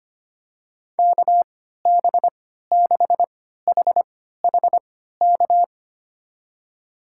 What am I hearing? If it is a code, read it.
Morse code, heard as a keyed tone, about 25 wpm, spaced as for 14 wpm, 700 Hz. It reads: KB6HHK
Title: USA Call Signs - 6 Characters - 3x Character Spacing - 25wpm